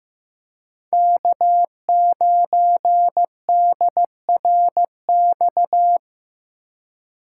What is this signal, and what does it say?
Morse code, heard as a keyed tone, about 15 wpm, 700 Hz. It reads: K9DRX
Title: USA Call Signs - 5 Characters - 4x Character Spacing - 15wpm